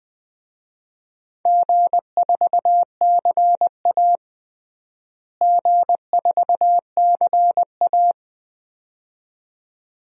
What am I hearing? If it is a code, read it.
Morse code, heard as a keyed tone, about 21 wpm, 700 Hz. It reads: G4CA G4CA
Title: DXCC-100 Call Signs - 4 Characters - Repeated 2x - 20wpm